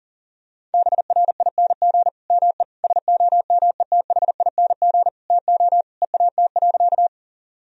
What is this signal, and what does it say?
Morse code, heard as a keyed tone, about 37 wpm, 700 Hz. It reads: BRING ME SOMETHING TO EAT.
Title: Sentences from Top 1000 Words - Review Part 3 - 40wpm